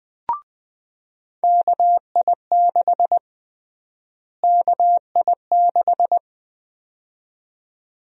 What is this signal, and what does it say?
Morse code, heard as a keyed tone, about 20 wpm, 700 Hz. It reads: KI6 KI6
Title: USA Call Sign Prefixes - Repeated 2x - 20wpm